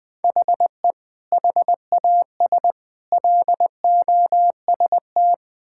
Morse code, heard as a keyed tone, about 20 wpm, 700 Hz. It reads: HE HAS LOST
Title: Sentences from Top 1600 Words 20wpm